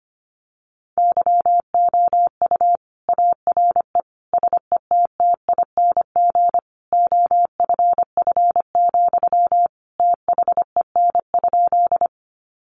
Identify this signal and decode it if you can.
Morse code, heard as a keyed tone, about 25 wpm, 700 Hz. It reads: YOU ARE SETTING OFF, THEN?
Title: Sentences from Top 1300 Words 25wpm